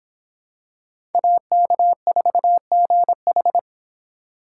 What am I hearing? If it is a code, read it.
Morse code, heard as a keyed tone, about 25 wpm, 700 Hz. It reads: AK4GH